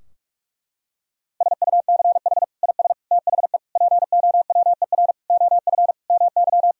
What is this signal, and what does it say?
Morse code, heard as a keyed tone, about 46 wpm, 700 Hz. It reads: SUCH IS THE POWER OF MY
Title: Sentences from Top 1000 Words - Review Part 2 - 45wpm